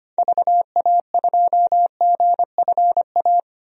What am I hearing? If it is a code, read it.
Morse code, heard as a keyed tone, about 25 wpm, 700 Hz. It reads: VA2GFA
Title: DXCC-100 Call Signs - 6 Characters - 25wpm